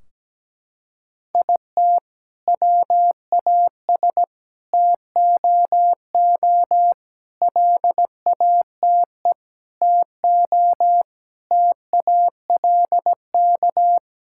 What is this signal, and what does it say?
Morse code, heard as a keyed tone, about 17 wpm, 700 Hz. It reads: IT WAS TOO LATE TO TALK
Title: Sentences from Top 1600 Words 17wpm